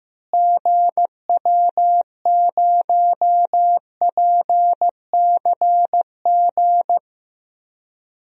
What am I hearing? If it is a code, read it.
Morse code, heard as a keyed tone, about 15 wpm, 700 Hz. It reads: GW0PCG
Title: DXCC-100 Call Signs - 6 Characters - Repeated 3x - 15wpm